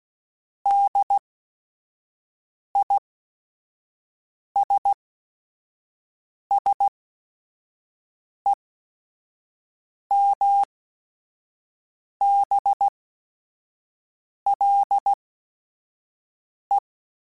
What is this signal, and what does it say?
Morse code, heard as a keyed tone, about 16 wpm, spaced as for 5 wpm, 800 Hz. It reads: DISSEMBLE